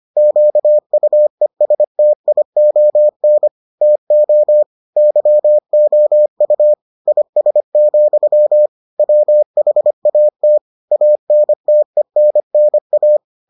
Morse code, heard as a keyed tone, about 25 wpm, 600 Hz. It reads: QUESTION TO YOU IS, WHAT ANTENNA